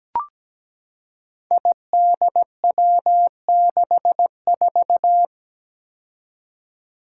Morse code, heard as a keyed tone, about 17 wpm, 700 Hz. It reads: IDW64
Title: Five Challenging Letter-Number Combinations 17wpm